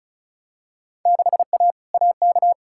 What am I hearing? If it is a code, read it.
Morse code, heard as a keyed tone, about 36 wpm, 700 Hz. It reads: 6A AK
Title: Field Day Exchanges Encore - 35wpm